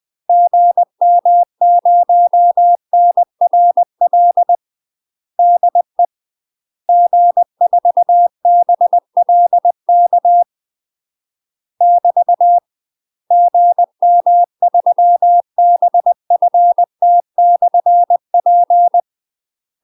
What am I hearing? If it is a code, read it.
Morse code, heard as a keyed tone, about 20 wpm, 700 Hz. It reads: GM0NRL DE G4BLK <BT> GM3BFT/P
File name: UK MASTER QSOs TRAINING / 20WPM / UK MASTER QSOs 0022